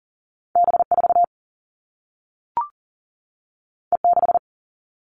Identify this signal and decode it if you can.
Morse code, heard as a keyed tone, about 38 wpm, 700 Hz. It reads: B4 E6
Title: All Cut Numbers - 2-Digit 40wpm